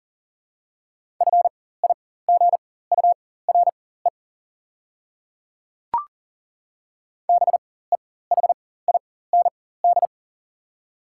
Morse code, heard as a keyed tone, about 40 wpm, spaced as for 19 wpm, 700 Hz. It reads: FIGURE BEHIND
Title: 4x Character Spacing - Top 500 Words - 40wpm